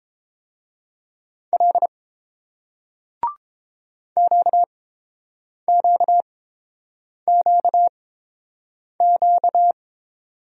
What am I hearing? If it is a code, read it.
Morse code, heard as a keyed tone, about 30 wpm, spaced as for 7 wpm, 690 Hz. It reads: L QQQQ